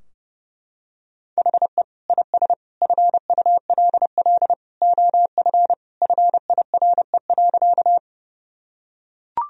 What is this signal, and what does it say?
Morse code, heard as a keyed tone, about 30 wpm, 700 Hz. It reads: HE IS FULL OF FIRE.
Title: Sentences from Top 1000 Words - Review Part 3 - 30wpm